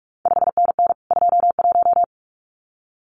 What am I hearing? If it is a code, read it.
Morse code, heard as a keyed tone, about 46 wpm, 700 Hz. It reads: 5NN 21
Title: CQ WW DX Contest Exchange Encore - 45wpm